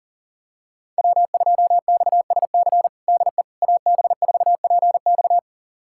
Morse code, heard as a keyed tone, about 38 wpm, 700 Hz. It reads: W2XSC DE AB4PX
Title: USA Callsign DE Callsign - 5 Characters - 40wpm